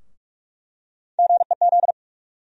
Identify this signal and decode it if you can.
Morse code, heard as a keyed tone, about 43 wpm, 700 Hz. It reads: GEZ
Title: USA Call Sign Postfixes - 3 Letters 45wpm